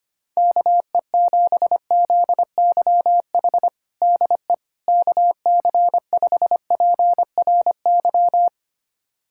Morse code, heard as a keyed tone, about 25 wpm, 700 Hz. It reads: KE7ZYH DE KC5PRY
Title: USA Callsign DE Callsign - 6 Characters - 25wpm